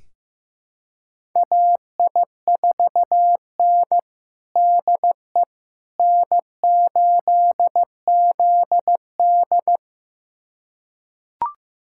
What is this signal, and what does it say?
Morse code, heard as a keyed tone, about 15 wpm, 700 Hz. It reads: AI4N DE N8ZD
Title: USA Callsign DE Callsign - 4 Characters - 15wpm